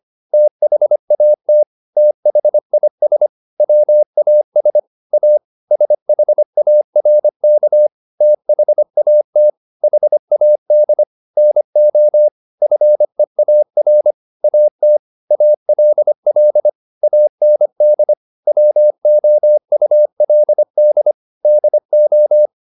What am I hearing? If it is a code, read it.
Morse code, heard as a keyed tone, about 25 wpm, 600 Hz. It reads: THAT THIS WAS A SHARK THAT HAD NO FEAR AT ALL AND WOULD DO